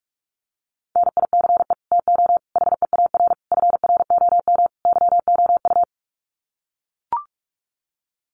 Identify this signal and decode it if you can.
Morse code, heard as a keyed tone, about 44 wpm, 700 Hz. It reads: NICE TO HEAR FROM YOU
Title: Common QSO Phrases 45wpm